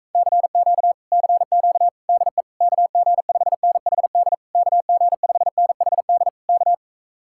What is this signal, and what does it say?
Morse code, heard as a keyed tone, about 45 wpm, 700 Hz. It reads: CQ CQ DE KG5NHD KG5NHD K